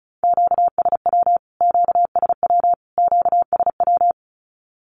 Morse code, heard as a keyed tone, about 34 wpm, 700 Hz. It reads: QSW QSW QSW
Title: Q-Codes Repeated 3x 35wpm